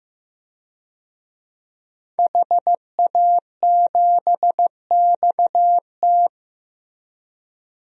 Morse code, heard as a keyed tone, about 15 wpm, 700 Hz. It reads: HA7XT
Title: DXCC-100 Call Signs - 5 Characters - Repeated 2x - 15wpm